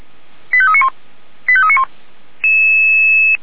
Five-tone selective call (ZVEI), heard as quick runs of five tones, tones of about 70 ms. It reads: 85391 85391
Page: /464/SDR/ZVEI/89263/